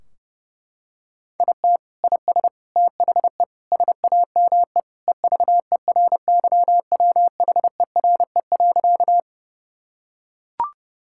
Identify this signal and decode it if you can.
Morse code, heard as a keyed tone, about 30 wpm, 700 Hz. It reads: IT IS THE SAME EVERYWHERE.
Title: Sentences from Top 1600 Words 30wpm